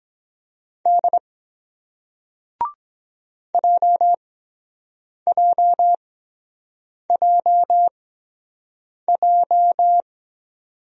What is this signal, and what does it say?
Morse code, heard as a keyed tone, about 22 wpm, spaced as for 7 wpm, 700 Hz. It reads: D JJJJ